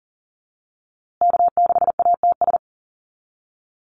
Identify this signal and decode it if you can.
Morse code, heard as a keyed tone, about 40 wpm, 700 Hz. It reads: K6ATS
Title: USA Call Signs Repeated 2x 40wpm